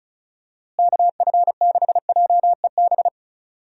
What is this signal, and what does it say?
Morse code, heard as a keyed tone, about 36 wpm, 700 Hz. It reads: KF6JEB
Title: USA Call Signs - 6 Characters - Repeated 2x - 35wpm